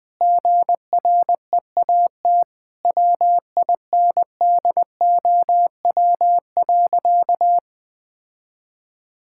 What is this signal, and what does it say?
Morse code, heard as a keyed tone, about 20 wpm, 700 Hz. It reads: GREAT WINDOW.
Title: Sentences from Top 1600 Words 20wpm